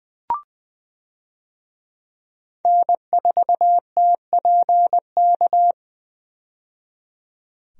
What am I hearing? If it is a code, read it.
Morse code, heard as a keyed tone, about 20 wpm, 700 Hz. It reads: N4TPK
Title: USA Call Signs - 5 Characters - 20wpm